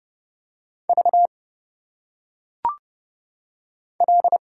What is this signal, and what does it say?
Morse code, heard as a keyed tone, about 29 wpm, 700 Hz. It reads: V L
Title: Single Letters - Phonetic Pronunciation 30wpm